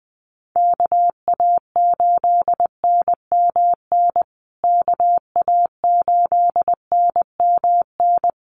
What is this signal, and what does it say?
Morse code, heard as a keyed tone, about 20 wpm, 700 Hz. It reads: KA8NMN KA8NMN
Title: USA Call Signs Repeated 3x 20wpm